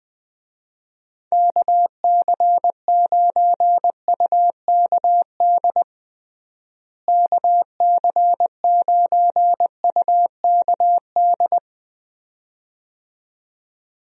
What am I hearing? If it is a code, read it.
Morse code, heard as a keyed tone, about 20 wpm, 700 Hz. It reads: KC9UKD KC9UKD
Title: USA Call Signs - 6 Characters - Repeated 2x - 20wpm